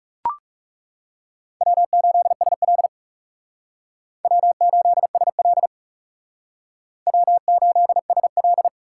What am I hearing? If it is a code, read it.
Morse code, heard as a keyed tone, about 39 wpm, 700 Hz. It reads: W8SL W8SL W8SL